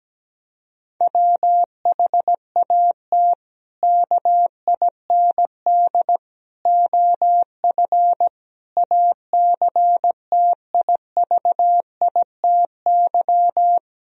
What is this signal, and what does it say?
Morse code, heard as a keyed tone, about 17 wpm, 700 Hz. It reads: WHAT KIND OF ACTIVITY
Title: Sentences from Top 1000 Words - Review Part 4 - 17wpm